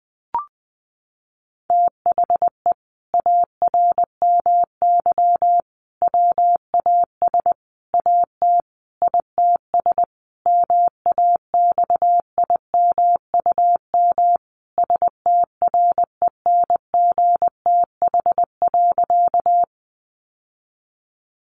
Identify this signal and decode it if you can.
Morse code, heard as a keyed tone, about 20 wpm, 700 Hz. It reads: THE ARMY WAS AT ITS MAXIMUM STRENGTH.
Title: Sentences from Top 1300 Words 20wpm